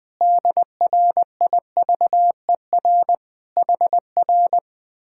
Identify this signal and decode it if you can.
Morse code, heard as a keyed tone, about 20 wpm, 700 Hz. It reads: DRIVER HR
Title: Speed-Racing Common QSO Phrases Encore - 20wpm